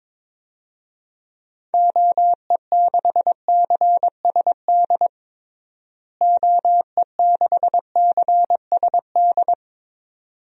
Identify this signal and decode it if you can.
Morse code, heard as a keyed tone, about 22 wpm, 700 Hz. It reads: OE6CSD OE6CSD
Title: DXCC-100 Call Signs - 6 Characters - Repeated 2x - 22wpm